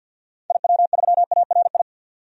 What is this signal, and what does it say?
Morse code, heard as a keyed tone, about 50 wpm, 700 Hz. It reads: IW3ARI
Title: DXCC-100 Call Signs - 6 Characters - 50wpm